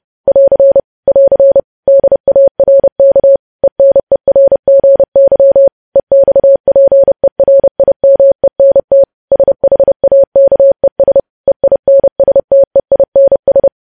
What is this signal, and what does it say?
Morse code, heard as a keyed tone, about 30 wpm, 550 Hz. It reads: <AR> <AR> DARK ENERGY EXPERIMENT SHAKES EINSTEINS
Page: news headlines in morse code at 30 wpm